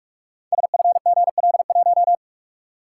Morse code, heard as a keyed tone, about 44 wpm, 700 Hz. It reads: SFGL1
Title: Code Groups - 5 Letter:Numbers - 45wpm